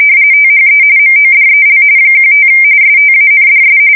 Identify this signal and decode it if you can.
RTTY, about 45.45 baud. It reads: THE MODEL 15 STANDS
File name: fsk170 / teletype-corp